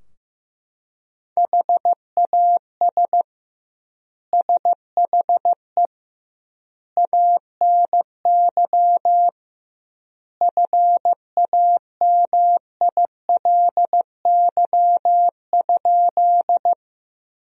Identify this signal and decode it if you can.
Morse code, heard as a keyed tone, about 15 wpm, 700 Hz. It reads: HAS SHE ANY FAMILY?
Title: 2x Word-Spacing - Sentences from Top 400 Words - 15wpm